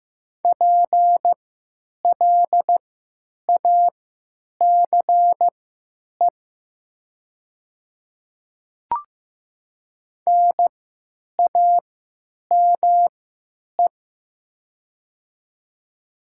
Morse code, heard as a keyed tone, about 15 wpm, spaced as for 9 wpm, 700 Hz. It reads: PLACE NAME
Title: 3x Character Spacing - Top 300 Words - 15wpm